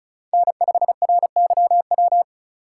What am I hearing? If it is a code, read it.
Morse code, heard as a keyed tone, about 34 wpm, 700 Hz. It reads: N5RYW
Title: USA Call Signs - 5 Characters - 3x Character Spacing - 35wpm